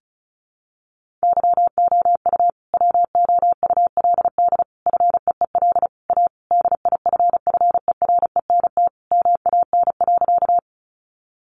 Morse code, heard as a keyed tone, about 34 wpm, 700 Hz. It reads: YOU WOULD FEEL A DIFFERENT MAN.
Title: Sentences from Top 600 Words 35wpm